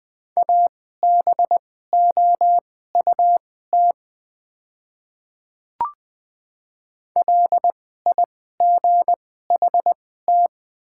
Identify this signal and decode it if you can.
Morse code, heard as a keyed tone, about 20 wpm, spaced as for 15 wpm, 690 Hz. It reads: ABOUT LIGHT
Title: 2x Character Spacing - Top 300 Words - 20wpm